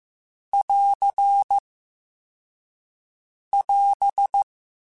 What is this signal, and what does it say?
Morse code, heard as a keyed tone, about 15 wpm, 800 Hz. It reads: <AR> <AS>
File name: Quotes Spaced / Quotes Spaced-mp3s / Airtight sp @10